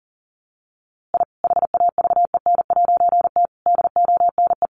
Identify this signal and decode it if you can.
Morse code, heard as a keyed tone, about 40 wpm, 700 Hz. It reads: I HAVEN'T DONE
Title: A Sound Of Thunder 40wpm